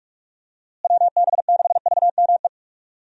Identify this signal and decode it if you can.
Morse code, heard as a keyed tone, about 46 wpm, 700 Hz. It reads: WB6VME
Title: USA Call Signs - 6 Characters - Repeated 2x - 45wpm